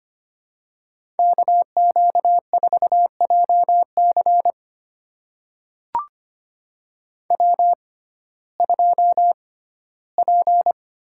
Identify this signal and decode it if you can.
Morse code, heard as a keyed tone, about 25 wpm, spaced as for 15 wpm, 700 Hz. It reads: KQ4JC W2P
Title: USA Call Signs - 5 Characters - 6x Character Spacing - 25wpm